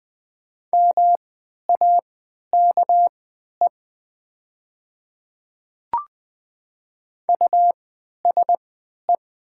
Morse code, heard as a keyed tone, about 20 wpm, spaced as for 11 wpm, 700 Hz. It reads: MAKE USE
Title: 3x Character Spacing - Top 200 Words - 20wpm